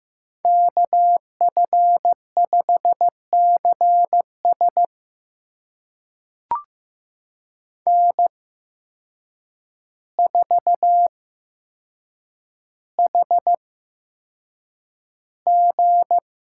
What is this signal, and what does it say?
Morse code, heard as a keyed tone, about 15 wpm, spaced as for 7 wpm, 700 Hz. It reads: KF5CS N4HG
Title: USA Call Signs - 8x Character Spacing - 15wpm